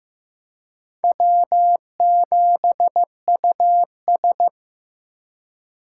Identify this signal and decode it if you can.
Morse code, heard as a keyed tone, about 15 wpm, 700 Hz. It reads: W7US
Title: USA Call Signs - 3x Character Spacing - 15wpm